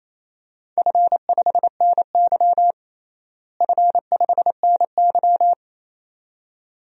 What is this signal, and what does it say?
Morse code, heard as a keyed tone, about 27 wpm, 700 Hz. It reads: F5NY F5NY
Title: DXCC-100 Call Signs - 4 Characters - Repeated 2x - 28wpm